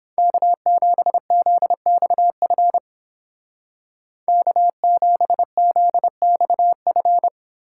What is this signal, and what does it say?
Morse code, heard as a keyed tone, about 30 wpm, 700 Hz. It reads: K7ZXF K7ZXF